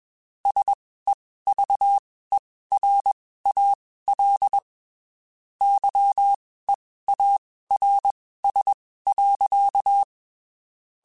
Morse code, heard as a keyed tone, about 21 wpm, spaced as for 15 wpm, 800 Hz. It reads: SEVERAL YEARS.
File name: (spaced) / QSOs Spaced-mp3s / kc7sac @20 sp